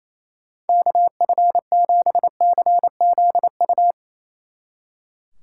Morse code, heard as a keyed tone, about 37 wpm, 700 Hz. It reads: KF7CZU